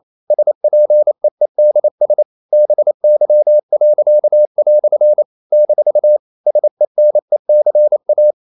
Morse code, heard as a keyed tone, about 28 wpm, 600 Hz. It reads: SPEEDS BY." - SENECA
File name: CW / Practice / EnglishQuotations0228